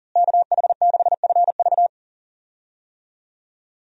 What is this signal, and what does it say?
Morse code, heard as a keyed tone, about 40 wpm, 700 Hz. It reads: KH6FV
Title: USA Call Signs - 5 Characters - Repeated 2x - 40wpm